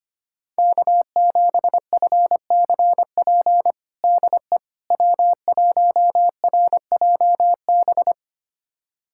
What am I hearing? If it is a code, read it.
Morse code, heard as a keyed tone, about 25 wpm, 700 Hz. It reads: K7FCP DE W1RJB
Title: USA Callsign DE Callsign - 5 Characters - 25wpm